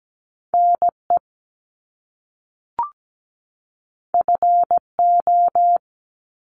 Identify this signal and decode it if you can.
Morse code, heard as a keyed tone, about 17 wpm, 700 Hz. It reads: NE FO
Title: Top Two Letter Combinations in Words 17wpm